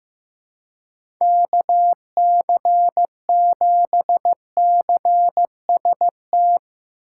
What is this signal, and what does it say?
Morse code, heard as a keyed tone, about 15 wpm, 700 Hz. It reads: KC7CST
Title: USA Call Signs - 6 Characters - 8x Character Spacing - 15wpm